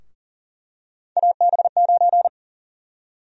Morse code, heard as a keyed tone, about 40 wpm, 700 Hz. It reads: AB9
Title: USA Call Sign Prefixes - Repeated 2x - 40wpm